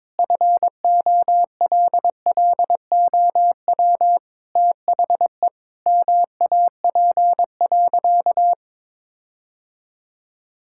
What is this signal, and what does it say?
Morse code, heard as a keyed tone, about 21 wpm, 700 Hz. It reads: FOLLOW THE MAP.
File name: practice / 22wpm / Sentences with 8-Word Phrases - 22wpm